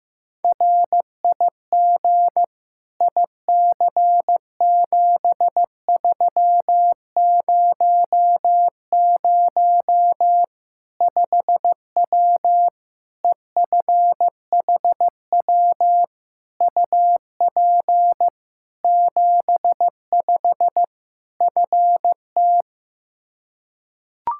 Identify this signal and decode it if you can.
Morse code, heard as a keyed tone, about 15 wpm, 700 Hz. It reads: RIG IC7300 5W EFHW UP 75 FT
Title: Practice QSO - Exchange 2 - Info - 15wpm